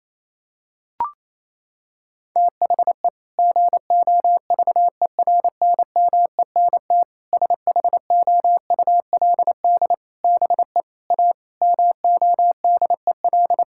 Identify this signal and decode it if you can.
Morse code, heard as a keyed tone, about 27 wpm, 700 Hz. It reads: THE GOVERNMENT SHOULD BE A MODEL
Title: Sentences from Top 1600 Words 28wpm